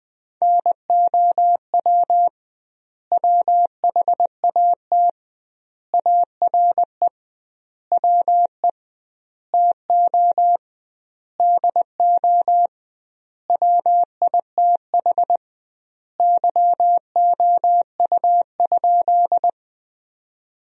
Morse code, heard as a keyed tone, about 20 wpm, 700 Hz. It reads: NOW WHAT ARE WE TO DO WITH YOU?
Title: 2x Word-Spacing - Sentences from Top 100 Words - 20wpm